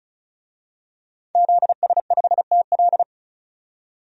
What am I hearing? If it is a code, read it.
Morse code, heard as a keyed tone, about 36 wpm, 700 Hz. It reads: ZS5TL
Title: DXCC-100 Call Signs - 5 Characters - 35wpm